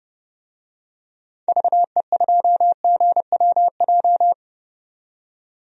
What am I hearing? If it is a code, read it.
Morse code, heard as a keyed tone, about 30 wpm, 700 Hz. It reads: VE2GWJ
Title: DXCC-100 Call Signs - 6 Characters - 30wpm